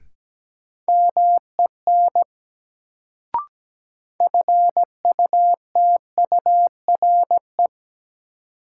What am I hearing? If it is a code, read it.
Morse code, heard as a keyed tone, about 17 wpm, 700 Hz. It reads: MEN FUTURE
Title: Sets of 3 Words - New Word Drill - 17wpm